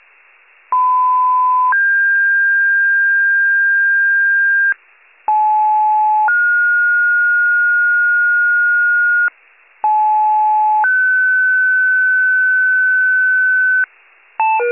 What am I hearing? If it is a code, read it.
Two-tone page, heard as a one-second tone, then a three-second tone, about 1010 then 1600 Hz, about 860 then 1430 Hz, about 860 then 1500 Hz. It is heard from 0.7 to 13.8 s.